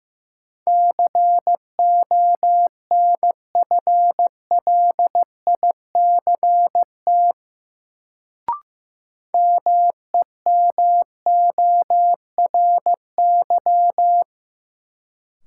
Morse code, heard as a keyed tone, about 15 wpm, 700 Hz. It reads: CONFLICT MEMORY
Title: Sentences from Top 1300 Words - New Word Drill - 15wpm